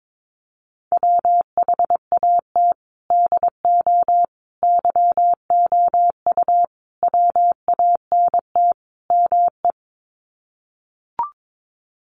Morse code, heard as a keyed tone, about 22 wpm, 700 Hz. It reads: WHAT DO YOU WANT ME
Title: Sets of 5 Words Encore2 22wpm